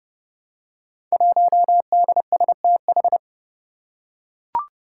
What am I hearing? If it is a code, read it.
Morse code, heard as a keyed tone, about 30 wpm, 700 Hz. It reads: 1DSTH
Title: Five Challenging Letter-Number Combinations 30wpm